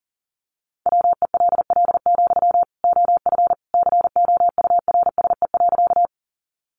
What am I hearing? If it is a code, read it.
Morse code, heard as a keyed tone, about 40 wpm, 700 Hz. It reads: WELL, OF COURSE.
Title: Sentences from Top 500 Words 40wpm